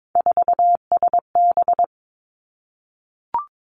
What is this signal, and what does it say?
Morse code, heard as a keyed tone, about 22 wpm, 700 Hz. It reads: 4SB
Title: Three Challenging Letter-Number Combinations 22wpm